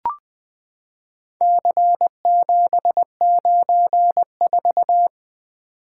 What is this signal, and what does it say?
Morse code, heard as a keyed tone, about 20 wpm, 700 Hz. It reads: C794
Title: Four Letter-Number Combinations - 20wpm